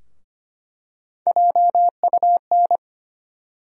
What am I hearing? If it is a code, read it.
Morse code, heard as a keyed tone, about 25 wpm, 700 Hz. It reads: JUN
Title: USA Call Sign Postfixes - 3 Letters 25wpm